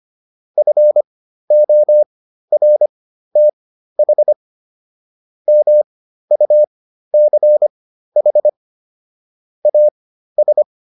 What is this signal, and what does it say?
Morse code, heard as a keyed tone, about 25 wpm, spaced as for 13 wpm, 600 Hz. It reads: FORTH MUCH AS